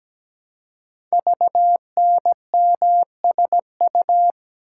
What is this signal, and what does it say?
Morse code, heard as a keyed tone, about 17 wpm, 700 Hz. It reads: VNMSU
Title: Code Groups - 5 Letters - 17wpm